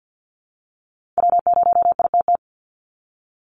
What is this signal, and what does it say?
Morse code, heard as a keyed tone, about 50 wpm, 700 Hz. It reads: W0ITT
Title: Call Signs Repeated 3x - 50wpm